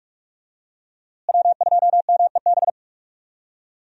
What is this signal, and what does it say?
Morse code, heard as a keyed tone, about 44 wpm, 700 Hz. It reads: W2MEB
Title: USA Call Signs - 5 Characters - Repeated 2x - 45wpm